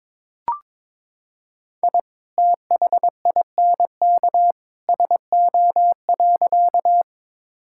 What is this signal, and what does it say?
Morse code, heard as a keyed tone, about 22 wpm, 700 Hz. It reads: I THINK SO.
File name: practice / 22wpm / Sentences with 2-Word Phrases - 22wpm